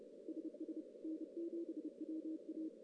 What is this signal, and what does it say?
Morse code, heard as a keyed tone, about 32 wpm, 320 Hz, with noise. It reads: SS N7WA
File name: CW / CW-SS-k5tr- -1700